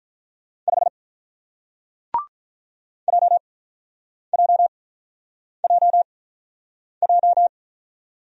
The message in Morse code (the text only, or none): 5 JJJJ